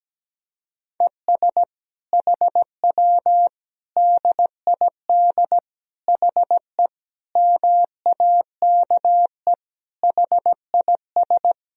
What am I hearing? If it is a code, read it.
Morse code, heard as a keyed tone, about 17 wpm, 700 Hz. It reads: ES HW DID HE MAKE HIS
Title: Sentences from Top 400 Words and Abbreviations 17wpm